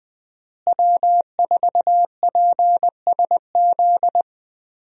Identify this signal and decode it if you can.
Morse code, heard as a keyed tone, about 20 wpm, 700 Hz. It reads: W4PSZ
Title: USA Call Signs - 5 Characters - Repeated 3x - 20wpm